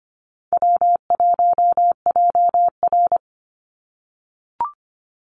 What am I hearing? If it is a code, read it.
Morse code, heard as a keyed tone, about 25 wpm, 700 Hz. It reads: W1JR
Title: Easy Call Signs - 25wpm